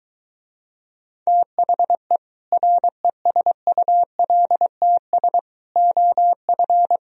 Morse code, heard as a keyed tone, about 23 wpm, 700 Hz. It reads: THE RESULTS OF